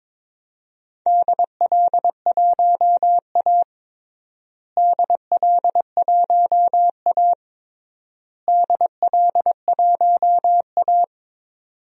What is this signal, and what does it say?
Morse code, heard as a keyed tone, about 22 wpm, 700 Hz. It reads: DL1A DL1A DL1A
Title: DXCC-100 Call Signs - 4 Characters - Repeated 3x - 22wpm